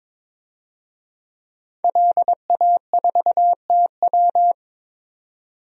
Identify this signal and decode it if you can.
Morse code, heard as a keyed tone, about 22 wpm, 700 Hz. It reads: LA4TW
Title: DXCC-100 Call Signs - 5 Characters - 22wpm